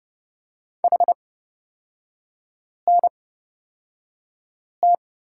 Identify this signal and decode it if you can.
Morse code, heard as a keyed tone, about 30 wpm, spaced as for 5 wpm, 690 Hz. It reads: HNT
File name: Lesson 37e - Letter x - Three Characters 30wpm